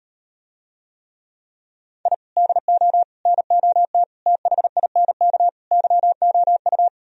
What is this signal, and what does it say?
Morse code, heard as a keyed tone, about 38 wpm, 700 Hz. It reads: I DO NOT THINK YOU